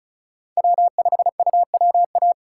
Morse code, heard as a keyed tone, about 34 wpm, 700 Hz. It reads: W5UWA
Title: USA Call Signs - 5 Characters - 6x Character Spacing - 35wpm